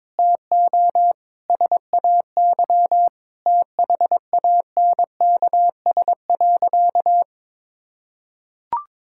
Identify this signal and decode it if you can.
Morse code, heard as a keyed tone, about 22 wpm, 700 Hz. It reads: TO SAY THANKS.
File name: practice / 22wpm / Sentences with 4-Word Phrases - 22wpm